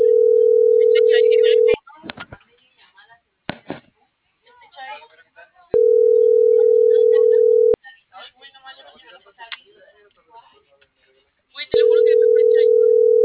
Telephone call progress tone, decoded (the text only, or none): ringback tone